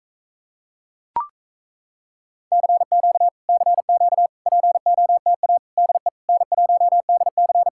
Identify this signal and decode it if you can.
Morse code, heard as a keyed tone, about 38 wpm, 700 Hz. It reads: CQ CQ POTA DE N1DC